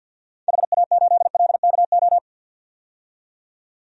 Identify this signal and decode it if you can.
Morse code, heard as a keyed tone, about 51 wpm, 700 Hz. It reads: HA8LKO
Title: DXCC-100 Call Signs - 6 Characters - 50wpm